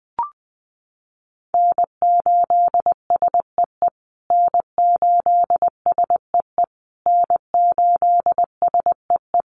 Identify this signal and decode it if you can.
Morse code, heard as a keyed tone, about 20 wpm, 700 Hz. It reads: N8SEE N8SEE N8SEE
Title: USA Call Signs Repeated 3x 20wpm